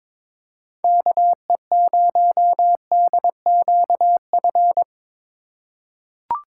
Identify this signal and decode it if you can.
Morse code, heard as a keyed tone, about 22 wpm, 700 Hz. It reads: KE0DQF
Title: USA Call Signs - 6 Characters - 2x Character Spacing - 22wpm